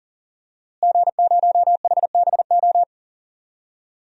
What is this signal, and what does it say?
Morse code, heard as a keyed tone, about 40 wpm, 700 Hz. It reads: G0HBO
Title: DXCC-100 Call Signs - 5 Characters - Repeated 3x - 40wpm